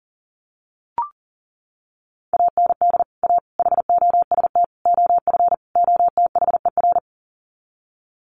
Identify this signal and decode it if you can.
Morse code, heard as a keyed tone, about 40 wpm, 700 Hz. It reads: AND A HOST OF OTHER
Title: Sets of 5 Words Encore2 40wpm